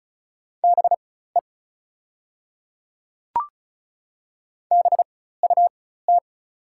Morse code, heard as a keyed tone, about 34 wpm, spaced as for 16 wpm, 690 Hz. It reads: BE BUT
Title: 4x Character Spacing - Top 200 Words - 35wpm